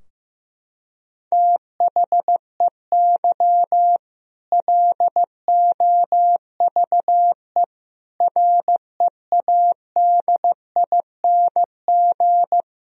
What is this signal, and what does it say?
Morse code, heard as a keyed tone, about 15 wpm, 700 Hz. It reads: THEY LOVE READING
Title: Sentences from Top 1000 Words - Review Part 3 - 15wpm